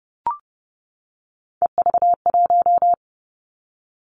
Morse code, heard as a keyed tone, about 30 wpm, 700 Hz. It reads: EV1
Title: Three Challenging Letter-Number Combinations 30wpm